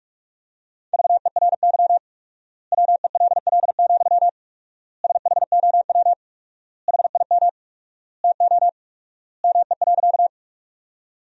Morse code, heard as a keyed tone, about 47 wpm, 700 Hz. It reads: VERY WELL, SHOW HIM TO ME.
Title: 4x Word-Spacing - Sentences from Top 200 Words - 45wpm